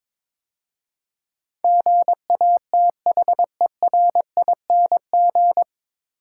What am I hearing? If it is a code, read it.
Morse code, heard as a keyed tone, about 22 wpm, 700 Hz. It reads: GATHERING
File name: 9 Letter Words - First 3 Letters Given - 22wpm